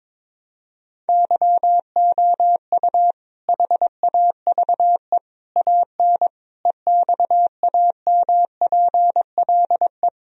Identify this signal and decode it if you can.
Morse code, heard as a keyed tone, about 22 wpm, 700 Hz. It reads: YOU HAVE AN EXAMPLE
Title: Sentences from Top 1000 Words - Review Part 1 - 22wpm